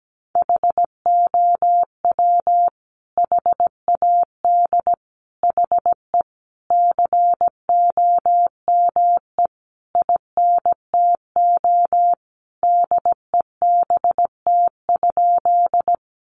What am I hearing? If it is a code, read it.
Morse code, heard as a keyed tone, about 17 wpm, 700 Hz. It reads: HOW HAD HE COME INTO DEBT?
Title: Sentences from Top 1300 Words 17wpm